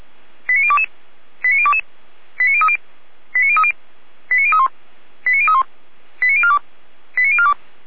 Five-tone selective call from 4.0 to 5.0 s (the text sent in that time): 89031